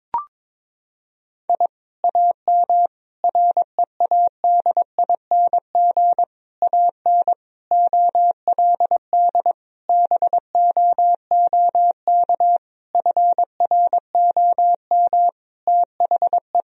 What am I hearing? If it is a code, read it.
Morse code, heard as a keyed tone, about 22 wpm, 700 Hz. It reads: I AM READING AN OLD BOOK FROM THE
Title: Sentences from Top 1000 Words - Review Part 4 - 22wpm